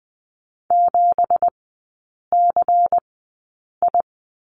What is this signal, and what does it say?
Morse code, heard as a keyed tone, about 20 wpm, spaced as for 8 wpm, 700 Hz. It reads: 7CI
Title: 166 Lesson 27e - Number 7 - Three Characters 20wpm